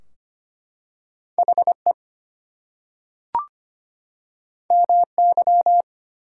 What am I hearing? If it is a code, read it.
Morse code, heard as a keyed tone, about 25 wpm, 700 Hz. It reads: HE MY
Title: Two Letter Words - 25wpm